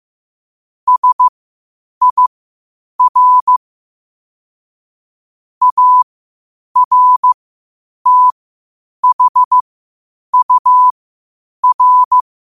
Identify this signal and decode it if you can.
Morse code, heard as a keyed tone, about 15 wpm, spaced as for 8 wpm, 1000 Hz. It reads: SIR ARTHUR